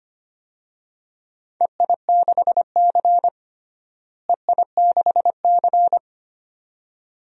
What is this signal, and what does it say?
Morse code, heard as a keyed tone, about 25 wpm, 700 Hz. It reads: EI6C EI6C
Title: DXCC-100 Call Signs - 4 Characters - Repeated 2x - 25wpm